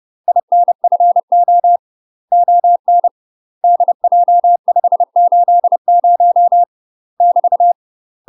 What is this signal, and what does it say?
Morse code, heard as a keyed tone, about 30 wpm, 700 Hz. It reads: INFO ON DJ580 <BT>
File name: UK MASTER QSOs TRAINING / 30WPM / UK MASTER QSOs 0008